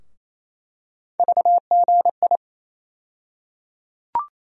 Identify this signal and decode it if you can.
Morse code, heard as a keyed tone, about 28 wpm, 700 Hz. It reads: VGI
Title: USA Call Sign Postfixes - 3 Letters 28wpm